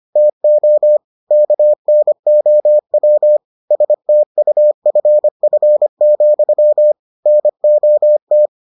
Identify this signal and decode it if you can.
Morse code, heard as a keyed tone, about 25 wpm, 600 Hz. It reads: TO KNOW STUFF, NOT